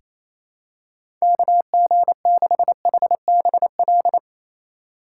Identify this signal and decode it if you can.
Morse code, heard as a keyed tone, about 28 wpm, 700 Hz. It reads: KG6HBL
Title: USA Call Signs - 2x Character Spacing - 28wpm